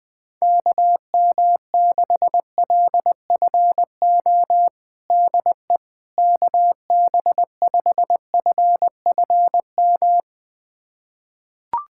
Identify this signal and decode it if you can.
Morse code, heard as a keyed tone, about 20 wpm, 700 Hz. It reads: KM6LFO DE KB5FFM
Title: USA Callsign DE Callsign - 6 Characters - 20wpm